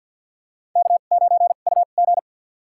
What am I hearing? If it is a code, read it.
Morse code, heard as a keyed tone, about 51 wpm, spaced as for 36 wpm, 700 Hz. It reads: K9UG
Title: USA Call Signs - 2x Character Spacing - 50wpm